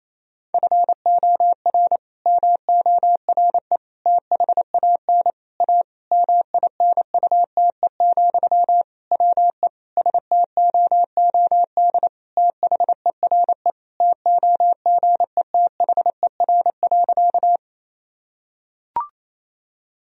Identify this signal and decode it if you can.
Morse code, heard as a keyed tone, about 28 wpm, 700 Hz. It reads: FOR MORE THAN A MINUTE, WE STOOD THERE TOGETHER.
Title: Sentences from Top 1000 Words - Review Part 2 - 28wpm